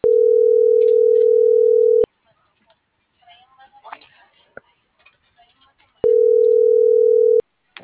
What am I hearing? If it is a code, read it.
Ringback tone (North America), 440 + 480 Hz.